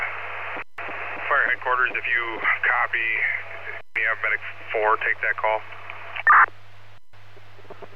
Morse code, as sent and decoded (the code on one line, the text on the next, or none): none